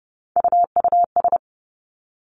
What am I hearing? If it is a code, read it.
Morse code, heard as a keyed tone, about 30 wpm, 700 Hz. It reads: UUS
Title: Three Challenging Letter-Number Combinations 30wpm